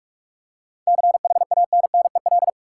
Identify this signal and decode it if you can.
Morse code, heard as a keyed tone, about 46 wpm, 700 Hz. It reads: CHANNEL
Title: Seven Letter Words - 45wpm